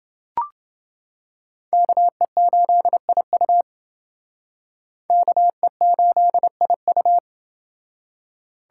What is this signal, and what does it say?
Morse code, heard as a keyed tone, about 29 wpm, 700 Hz. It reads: KE8IU KE8IU